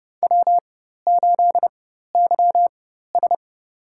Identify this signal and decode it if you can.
Morse code, heard as a keyed tone, about 32 wpm, spaced as for 14 wpm, 700 Hz. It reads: W8YS
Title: USA Call Signs - 4 Characters - 4x Character Spacing - 30wpm